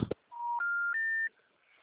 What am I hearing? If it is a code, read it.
Special information tone.
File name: out- -2020- - - .2